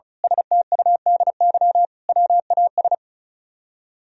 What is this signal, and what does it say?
Morse code, heard as a keyed tone, about 33 wpm, 700 Hz. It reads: STUDY WAS
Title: Speed-Racing - Sets of 5 Words 35wpm